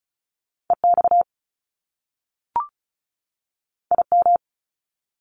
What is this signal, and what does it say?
Morse code, heard as a keyed tone, about 36 wpm, 700 Hz. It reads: EX IM
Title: Top Two Letter Combinations in Words 35wpm